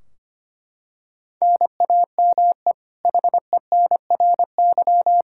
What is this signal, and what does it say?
Morse code, heard as a keyed tone, about 25 wpm, 700 Hz. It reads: NAME HENRY
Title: Practice QSO - Exchange 1 - Beginning - 25wpm